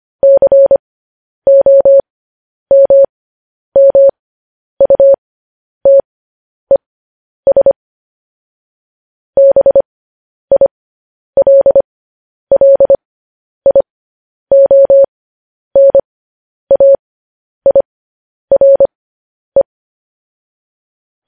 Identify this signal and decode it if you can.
Morse code, heard as a keyed tone, about 25 wpm, spaced as for 10 wpm, 550 Hz. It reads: COMMUTES BILLIONAIRE
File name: podcast / News headlines-1x-25x10